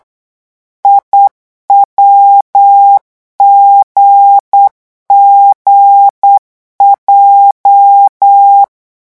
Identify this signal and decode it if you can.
Morse code, heard as a keyed tone, about 8 wpm, 800 Hz. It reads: IWGGJ
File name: morse mp3 / revision8-vitesse 8 mots